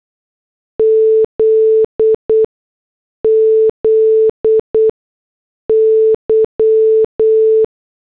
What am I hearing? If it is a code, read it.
Morse code, heard as a keyed tone, about 8 wpm, spaced as for 6 wpm, 440 Hz. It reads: ZZY